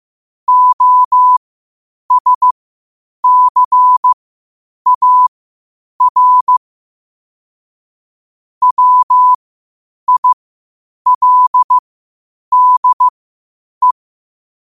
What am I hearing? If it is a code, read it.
Morse code, heard as a keyed tone, about 15 wpm, spaced as for 8 wpm, 1000 Hz. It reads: OSCAR WILDE
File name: qotdfiles / 8wpm